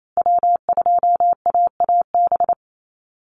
Morse code, heard as a keyed tone, about 28 wpm, 700 Hz. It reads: W2AAB
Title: Call Signs Repeated 2x - 28wpm